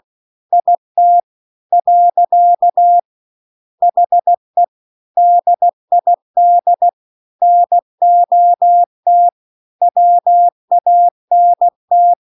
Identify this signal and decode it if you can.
Morse code, heard as a keyed tone, about 16 wpm, 700 Hz. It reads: IT . HE DID NOT WANT